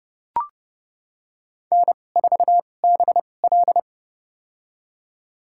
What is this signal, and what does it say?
Morse code, heard as a keyed tone, about 30 wpm, spaced as for 22 wpm, 700 Hz. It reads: N4BL
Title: USA Call Signs - 4 Characters - 2x Character Spacing - 30wpm